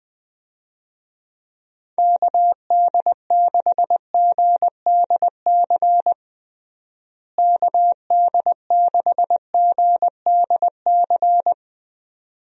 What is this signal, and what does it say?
Morse code, heard as a keyed tone, about 20 wpm, 700 Hz. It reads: KD6GDC KD6GDC
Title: USA Call Signs - 6 Characters - Repeated 2x - 20wpm